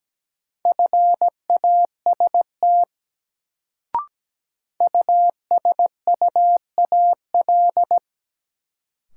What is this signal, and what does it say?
Morse code, heard as a keyed tone, about 17 wpm, 700 Hz. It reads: FAST USUAL
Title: Sentences from Top 400 Words - New Word Drill - 17wpm